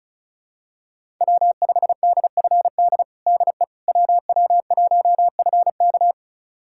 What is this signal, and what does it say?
Morse code, heard as a keyed tone, about 36 wpm, 700 Hz. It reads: W5DFD DE WW1FK
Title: USA Callsign DE Callsign - 5 Characters - 35wpm